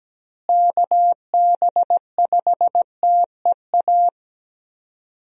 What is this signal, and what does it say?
Morse code, heard as a keyed tone, about 17 wpm, 700 Hz. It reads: KB5TEA
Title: USA Call Signs - 6 Characters - Repeated 3x - 17wpm